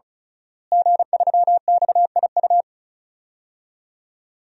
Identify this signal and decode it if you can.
Morse code, heard as a keyed tone, about 36 wpm, 700 Hz. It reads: G3XIU
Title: DXCC-100 Call Signs - 5 Characters - Repeated 2x - 35wpm